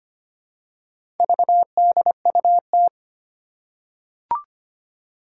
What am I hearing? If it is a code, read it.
Morse code, heard as a keyed tone, about 25 wpm, 700 Hz. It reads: VDUT
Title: Four Challenging Letter-Number Combinations 25wpm